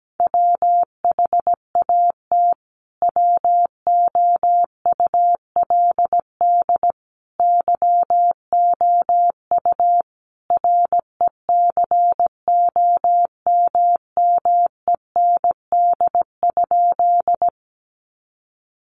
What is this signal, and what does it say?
Morse code, heard as a keyed tone, about 17 wpm, 700 Hz. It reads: WHAT WOULD YOU RECOMMEND?
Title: Sentences from Top 1200 Words 17wpm